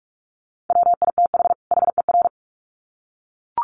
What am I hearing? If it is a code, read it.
Morse code, heard as a keyed tone, about 44 wpm, 700 Hz. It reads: WITH HER
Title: Sets of 2 Words 45wpm